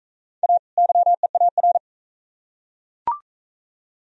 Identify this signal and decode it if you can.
Morse code, heard as a keyed tone, about 38 wpm, 700 Hz. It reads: A YEAR